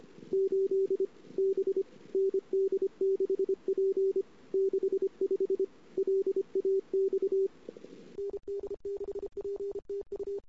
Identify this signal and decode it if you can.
Morse code, heard as a keyed tone, about 32 wpm, 380 Hz. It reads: 8 B ND6P 65 LAX ND6P TU
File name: CW / 2008-ss-cw-k5tr-op-n5rz -22:00